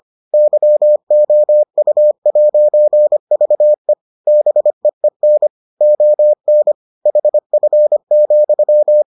Morse code, heard as a keyed tone, about 25 wpm, 600 Hz. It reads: YOU'VE BEEN ON HF,